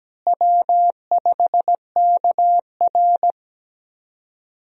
Morse code, heard as a keyed tone, about 17 wpm, 700 Hz. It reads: W5KR
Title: USA Call Signs - 4 Characters - Repeated 2x - 17wpm